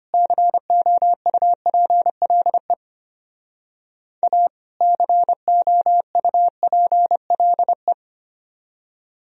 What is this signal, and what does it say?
Morse code, heard as a keyed tone, about 31 wpm, 700 Hz. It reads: COUPLE A COUPLE
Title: Speed-Racing - Sets of 2 Words 25wpm